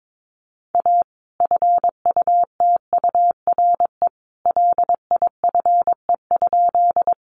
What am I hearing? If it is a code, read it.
Morse code, heard as a keyed tone, about 22 wpm, 700 Hz. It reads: A FUTURE LIFE?
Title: Sentences from Top 700 Words 22wpm